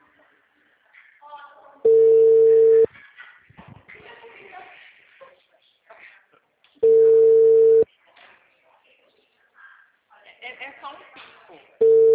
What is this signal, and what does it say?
Ringback tone (China, 450 Hz), with speech.